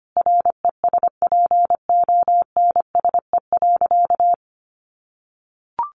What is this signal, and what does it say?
Morse code, heard as a keyed tone, about 25 wpm, 700 Hz. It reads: RESPONSE.
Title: Sentences from Top 1400 Words 25wpm